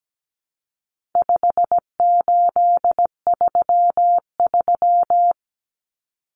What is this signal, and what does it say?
Morse code, heard as a keyed tone, about 17 wpm, 700 Hz. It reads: 5833
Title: Numbers - 4-Digit 17wpm